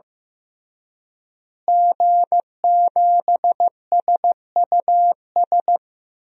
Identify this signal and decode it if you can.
Morse code, heard as a keyed tone, about 16 wpm, 700 Hz. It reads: G7SUS
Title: DXCC-100 Call Signs - 5 Characters - Repeated 3x - 15wpm